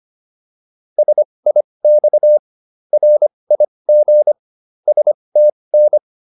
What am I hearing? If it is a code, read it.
Morse code, heard as a keyed tone, about 25 wpm, spaced as for 20 wpm, 600 Hz. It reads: SIX RIG STN